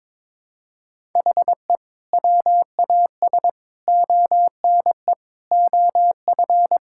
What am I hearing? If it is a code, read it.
Morse code, heard as a keyed tone, about 22 wpm, 700 Hz. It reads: HE WAS ONE OF
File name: practice / 22wpm / Build A Phrase - 5-Word Phrases - 22wpm